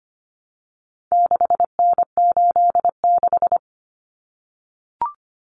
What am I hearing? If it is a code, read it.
Morse code, heard as a keyed tone, about 25 wpm, 700 Hz. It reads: 6N86
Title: Common Cut Numbers - 4-Digit 25wpm